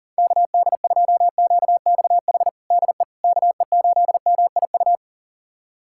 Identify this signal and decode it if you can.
Morse code, heard as a keyed tone, about 40 wpm, 700 Hz. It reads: KD2QXH DE KE8MIU
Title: USA Callsign DE Callsign - 6 Characters - 40wpm